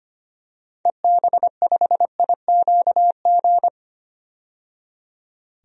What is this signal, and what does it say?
Morse code, heard as a keyed tone, about 25 wpm, 700 Hz. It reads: EB5IQG
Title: DXCC-100 Call Signs - 6 Characters - 25wpm